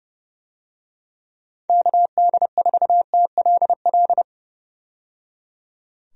Morse code, heard as a keyed tone, about 30 wpm, 700 Hz. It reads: KD4TLL
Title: USA Call Signs - 6 Characters - 30wpm